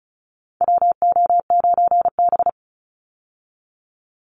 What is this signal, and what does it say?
Morse code, heard as a keyed tone, about 34 wpm, 700 Hz. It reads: WO9B
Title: Call Signs Repeated 3x - 35wpm